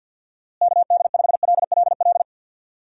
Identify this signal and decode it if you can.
Morse code, heard as a keyed tone, about 46 wpm, 700 Hz. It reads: KD5LLL
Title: USA Call Signs - 6 Characters - Repeated 3x - 50wpm